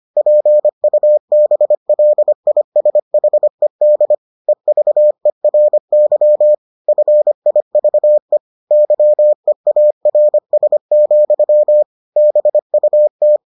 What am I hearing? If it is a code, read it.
Morse code, heard as a keyed tone, about 25 wpm, 600 Hz. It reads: PUBLISHED EVERY FIVE YEARS, BUT